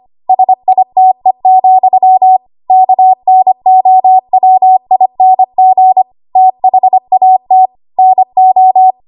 Morse code, heard as a keyed tone, about 25 wpm, 750 Hz. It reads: SITE, KNOWING THAT NO
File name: Morse / 25WPM